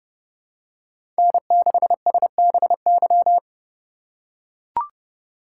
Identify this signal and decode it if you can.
Morse code, heard as a keyed tone, about 30 wpm, 700 Hz. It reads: N6SBY
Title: USA Call Signs - 5 Characters - 2x Character Spacing - 30wpm